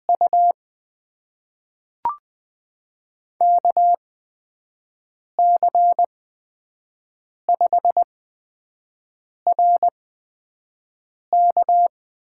Morse code, heard as a keyed tone, about 20 wpm, spaced as for 5 wpm, 700 Hz. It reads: U KC5RK
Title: USA Call Signs - 5 Characters - 8x Character Spacing - 20wpm